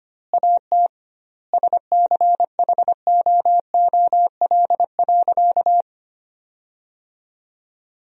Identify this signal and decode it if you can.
Morse code, heard as a keyed tone, about 25 wpm, 700 Hz. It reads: AT SCHOOL.
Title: 2x Word-Spacing - Sentences from Top 300 Words - 25wpm